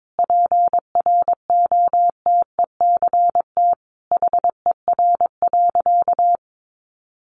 Morse code, heard as a keyed tone, about 22 wpm, 700 Hz. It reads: PROTECT HER.
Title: Sentences from Top 900 Words 22wpm